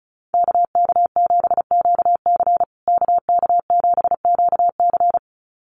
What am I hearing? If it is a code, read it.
Morse code, heard as a keyed tone, about 34 wpm, 700 Hz. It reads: KK7QC KK7QC
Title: USA Call Signs Repeated 2x 35wpm